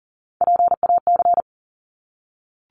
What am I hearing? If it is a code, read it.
Morse code, heard as a keyed tone, about 40 wpm, 700 Hz. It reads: PAC
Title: ARRL Sections 40wpm